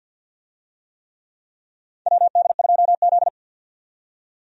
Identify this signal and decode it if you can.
Morse code, heard as a keyed tone, about 48 wpm, 700 Hz. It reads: WD2Z
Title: USA Call Signs - 4 Characters - 50wpm